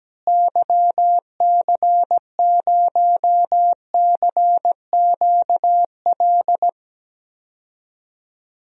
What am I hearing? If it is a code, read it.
Morse code, heard as a keyed tone, about 17 wpm, 700 Hz. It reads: YC0CQL
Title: DXCC-100 Call Signs - 6 Characters - Repeated 3x - 17wpm